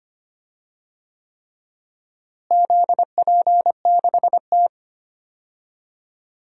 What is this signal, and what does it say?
Morse code, heard as a keyed tone, about 25 wpm, 700 Hz. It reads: ZP6T
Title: DXCC-100 Call Signs - 4 Characters - Repeated 3x - 25wpm